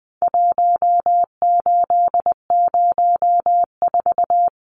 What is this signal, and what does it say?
Morse code, heard as a keyed tone, about 20 wpm, 700 Hz. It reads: 1804
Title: Numbers - Contest Style - 4-Digit 20wpm